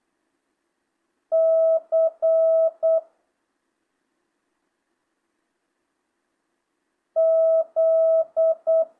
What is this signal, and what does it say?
Morse code, heard as a keyed tone, about 8 wpm, 650 Hz. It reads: C Z
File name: jh8cbh / 2025cw